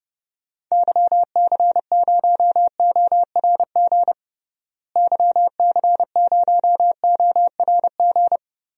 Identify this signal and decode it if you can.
Morse code, heard as a keyed tone, about 30 wpm, 700 Hz. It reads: YC0ORG YC0ORG
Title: DXCC-100 Call Signs - 6 Characters - Repeated 2x - 30wpm